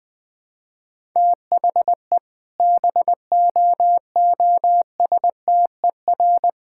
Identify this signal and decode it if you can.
Morse code, heard as a keyed tone, about 21 wpm, 700 Hz. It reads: THE BOOSTER
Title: All-Age Appropriate Jokes - 20wpm